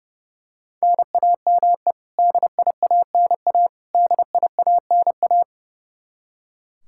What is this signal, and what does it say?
Morse code, heard as a keyed tone, about 30 wpm, 700 Hz. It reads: NAME DIANA DIANA
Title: Practice QSO - Exchange 1 - Beginning - 30wpm